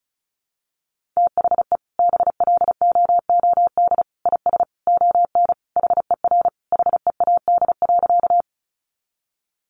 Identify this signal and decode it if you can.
Morse code, heard as a keyed tone, about 34 wpm, 700 Hz. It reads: THE BLOOD IS ON HER HEAD.
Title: Sentences from Top 1100 Words 35wpm